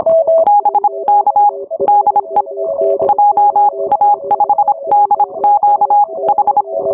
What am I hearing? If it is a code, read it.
Morse code, heard as a keyed tone, about 25 wpm, spaced as for 22 wpm, 820 Hz, with noise. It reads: BK DE JA5DQH
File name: onair